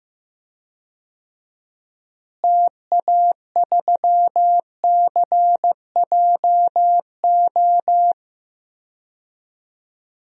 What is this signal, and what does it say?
Morse code, heard as a keyed tone, about 15 wpm, 700 Hz. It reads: TA3CJO
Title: DXCC-100 Call Signs - 6 Characters - Repeated 3x - 15wpm